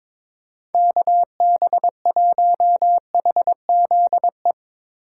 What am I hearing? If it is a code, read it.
Morse code, heard as a keyed tone, about 22 wpm, 700 Hz. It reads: KB1HZE
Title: USA Call Signs - 6 Characters - 2x Character Spacing - 22wpm